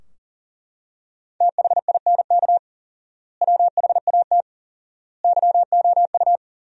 Morse code, heard as a keyed tone, about 37 wpm, 700 Hz. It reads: THINK WHAT YOU
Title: 4x Word-Spacing - Sentences from Top 200 Words - 40wpm